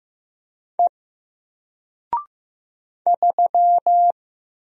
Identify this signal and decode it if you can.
Morse code, heard as a keyed tone, about 15 wpm, 700 Hz. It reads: E 3
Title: Single Letter-Number 15wpm